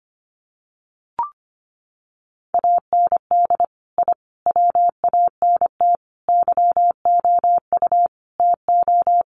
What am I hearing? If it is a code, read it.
Morse code, heard as a keyed tone, about 25 wpm, 700 Hz. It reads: AND I WANT YOU TO
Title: Sets of 5 Words Encore 25wpm